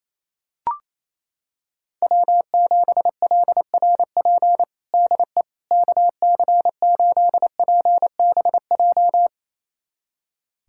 Morse code, heard as a keyed tone, about 28 wpm, 700 Hz. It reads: W7LRP DE KC8PBJ
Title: USA Callsign DE Callsign 28wpm